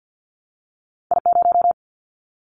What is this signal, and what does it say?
Morse code, heard as a keyed tone, about 50 wpm, 700 Hz. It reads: I0
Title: Callsign Prefixes for Top 25 DXCC - 50wpm